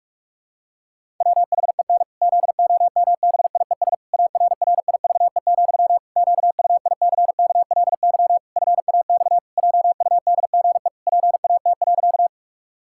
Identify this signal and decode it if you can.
Morse code, heard as a keyed tone, about 46 wpm, 700 Hz. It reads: WHEN ZOMBIES ARRIVE, QUICKLY FAX JUDGE PAT.
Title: Pangram Challenge v2 - 45wpm